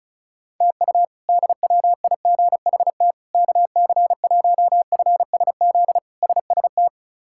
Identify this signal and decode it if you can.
Morse code, heard as a keyed tone, about 34 wpm, 700 Hz. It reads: TU DWIGHT KC1FSZ SST
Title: K1USN SST QSO Practice - 35wpm